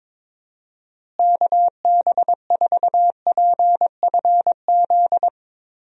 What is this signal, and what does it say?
Morse code, heard as a keyed tone, about 22 wpm, 700 Hz. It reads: KB4PFZ
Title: USA Call Signs - 6 Characters - Repeated 2x - 22wpm